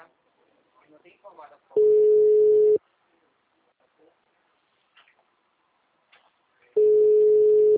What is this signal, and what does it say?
Ringback tone (Europe, 425 Hz), with speech.